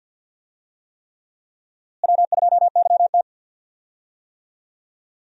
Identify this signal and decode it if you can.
Morse code, heard as a keyed tone, about 47 wpm, 700 Hz. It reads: W2YT
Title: USA Call Signs - 4 Characters - Repeated 3x - 50wpm